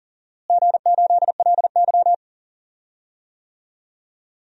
Morse code, heard as a keyed tone, about 39 wpm, 700 Hz. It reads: G8LY
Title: DXCC-100 Call Signs - 4 Characters - Repeated 3x - 40wpm